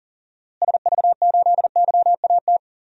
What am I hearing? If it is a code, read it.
Morse code, heard as a keyed tone, about 40 wpm, 700 Hz. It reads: SV8YAT
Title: DXCC-100 Call Signs - 6 Characters - Repeated 2x - 40wpm